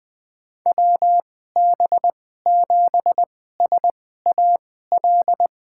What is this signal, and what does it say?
Morse code, heard as a keyed tone, about 20 wpm, spaced as for 14 wpm, 700 Hz. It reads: WB7SAL
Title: USA Call Signs - 6 Characters - 2x Character Spacing - 20wpm